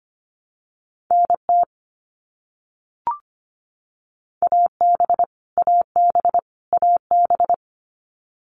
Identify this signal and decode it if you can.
Morse code, heard as a keyed tone, about 25 wpm, 700 Hz. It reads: NT AB AB AB
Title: Canadian Provinces and Territories - Repeated 3x - 25wpm